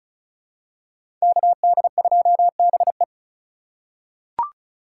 Morse code, heard as a keyed tone, about 34 wpm, 700 Hz. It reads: KD2BE
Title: USA Call Signs - 5 Characters - 3x Character Spacing - 35wpm